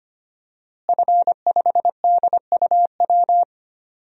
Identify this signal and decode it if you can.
Morse code, heard as a keyed tone, about 25 wpm, 700 Hz. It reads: F5DUW
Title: DXCC-100 Call Signs - 5 Characters - Repeated 2x - 25wpm